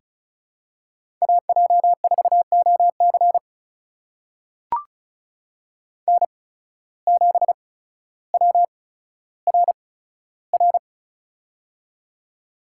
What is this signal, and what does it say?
Morse code, heard as a keyed tone, about 34 wpm, spaced as for 15 wpm, 700 Hz. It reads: AJ4OC N7WRR
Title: USA Call Signs - 8x Character Spacing - 35wpm